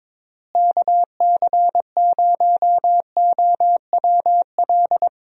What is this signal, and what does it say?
Morse code, heard as a keyed tone, about 22 wpm, 700 Hz. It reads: KC0OWL
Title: USA Call Signs - 6 Characters - Repeated 3x - 22wpm